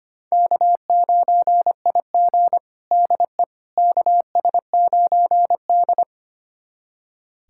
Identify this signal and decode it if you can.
Morse code, heard as a keyed tone, about 25 wpm, 700 Hz. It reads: K9IG DE KS9D
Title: USA Callsign DE Callsign - 4 Characters - 25wpm